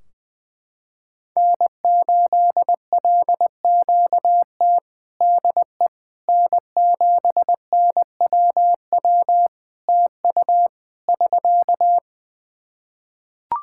Morse code, heard as a keyed tone, about 20 wpm, 700 Hz. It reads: N8LQT DE N7NWW TU <SK>
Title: Practice QSO - Exchange 4 - Ending - 20wpm